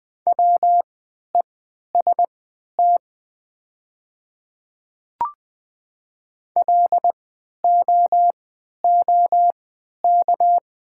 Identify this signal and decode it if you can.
Morse code, heard as a keyed tone, about 20 wpm, spaced as for 11 wpm, 700 Hz. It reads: WEST LOOK
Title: 3x Character Spacing - Top 500 Words - 20wpm